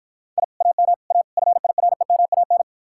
Morse code, heard as a keyed tone, about 53 wpm, 700 Hz. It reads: I AM A FIREMAN